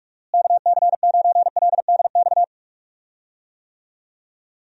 Morse code, heard as a keyed tone, about 46 wpm, 700 Hz. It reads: KC9LDX
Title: USA Call Signs - 6 Characters - Repeated 3x - 45wpm